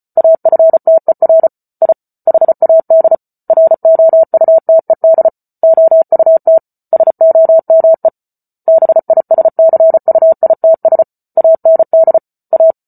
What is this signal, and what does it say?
Morse code, heard as a keyed tone, about 34 wpm, 650 Hz. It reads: AFTER I HAD ROUTED OUT SOME BISCUITS AND A